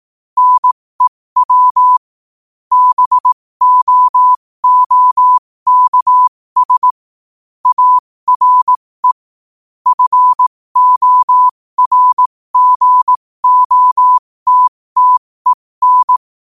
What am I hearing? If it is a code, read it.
Morse code, heard as a keyed tone, about 18 wpm, spaced as for 15 wpm, 1000 Hz. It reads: NEW BOOKS ARE FORGOTTEN
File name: qotdfiles / 15wpm